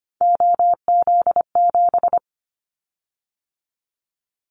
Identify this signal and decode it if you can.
Morse code, heard as a keyed tone, about 25 wpm, 700 Hz. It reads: OZ7
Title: Callsign Prefixes for Top 50 DXCC - 25wpm